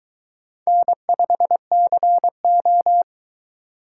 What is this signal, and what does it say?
Morse code, heard as a keyed tone, about 23 wpm, 700 Hz. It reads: N5CO